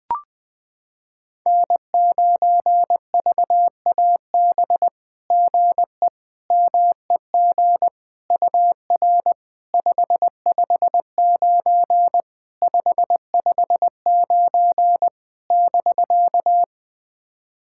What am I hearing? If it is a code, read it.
Morse code, heard as a keyed tone, about 20 wpm, 700 Hz. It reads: N9VAB GE MEG UR 559 559 <BK>
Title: SOTA QSO Practice 20wpm